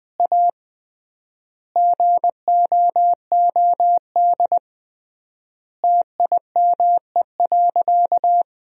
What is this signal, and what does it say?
Morse code, heard as a keyed tone, about 20 wpm, 700 Hz. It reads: A GOOD TIME.
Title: 3x Word-Spacing - Sentences from Top 200 Words - 20wpm